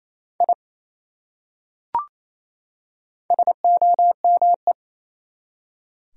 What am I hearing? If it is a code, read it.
Morse code, heard as a keyed tone, about 28 wpm, 700 Hz. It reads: I SOME
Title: Top 100 Words 28wpm